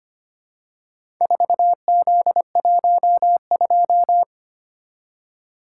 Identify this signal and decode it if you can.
Morse code, heard as a keyed tone, about 25 wpm, 700 Hz. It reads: 4Z12
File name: practice / 25wpm / Four Letter-Number Combinations - 25wpm